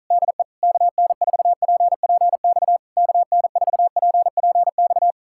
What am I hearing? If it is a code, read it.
Morse code, heard as a keyed tone, about 47 wpm, 700 Hz. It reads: DE KN4PPX KN4PPX